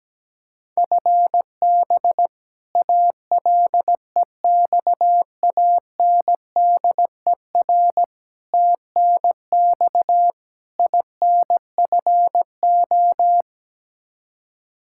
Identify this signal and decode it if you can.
Morse code, heard as a keyed tone, about 17 wpm, 700 Hz. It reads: FB ALEXANDER TNX INFO
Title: K1USN SST QSO Practice 17wpm